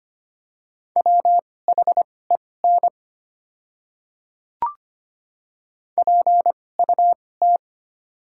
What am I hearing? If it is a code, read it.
Morse code, heard as a keyed tone, about 25 wpm, spaced as for 18 wpm, 700 Hz. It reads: WHEN PUT
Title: 2x Character Spacing - Top 200 Words - 25wpm